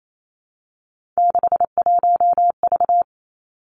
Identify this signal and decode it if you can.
Morse code, heard as a keyed tone, about 28 wpm, 700 Hz. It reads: 61V
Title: Three Challenging Letter-Number Combinations 28wpm